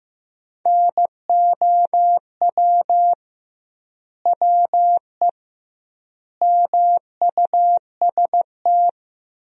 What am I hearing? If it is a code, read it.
Morse code, heard as a keyed tone, about 15 wpm, 700 Hz. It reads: NOW WE MUST
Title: 2x Word-Spacing - Sentences from Top 500 Words - 15wpm